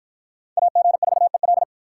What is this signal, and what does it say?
Morse code, heard as a keyed tone, about 44 wpm, 700 Hz. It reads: AG4EL